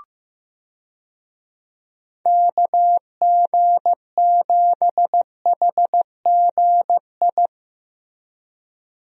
Morse code, heard as a keyed tone, about 15 wpm, 700 Hz. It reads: KG7HGI
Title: USA Call Signs - 6 Characters - Repeated 2x - 15wpm